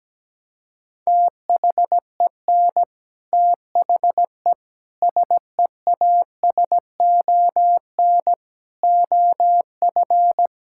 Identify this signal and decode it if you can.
Morse code, heard as a keyed tone, about 17 wpm, 700 Hz. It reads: THEN THE SEASON OF
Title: Sentences from Top 1600 Words 17wpm